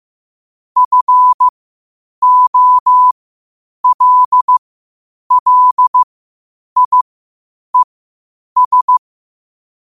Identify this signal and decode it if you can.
Morse code, heard as a keyed tone, about 15 wpm, spaced as for 8 wpm, 1000 Hz. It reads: FOLLIES
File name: qotdfiles / 8wpm